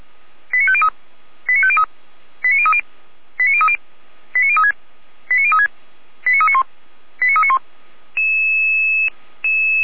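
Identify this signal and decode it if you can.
Five-tone selective call (ZVEI), heard as quick runs of five tones, tones of about 70 ms. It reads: 89503 89503 89030 89030 89036 89036 89391 89391
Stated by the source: ZVEI five-tone selective call received with SDR at the Orlamünde fire brigade (feuerwehr-orlamuende).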